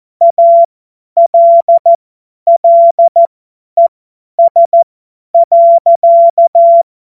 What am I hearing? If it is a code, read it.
Morse code, heard as a keyed tone, about 14 wpm, spaced as for 10 wpm, 680 Hz. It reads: ALLES.
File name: Morsekurs / Witze / Witz 26 März 2024 50 F70